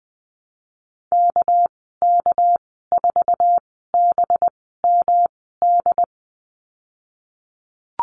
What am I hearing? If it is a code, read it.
Morse code, heard as a keyed tone, about 19 wpm, spaced as for 14 wpm, 700 Hz. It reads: KK4BMD
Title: 200 Lesson 32g - Letter k - Call Signs 20wpm